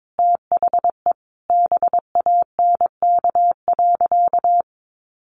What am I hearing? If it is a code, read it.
Morse code, heard as a keyed tone, about 22 wpm, 700 Hz. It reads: THE BANK.
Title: Sentences from Top 1200 Words 22wpm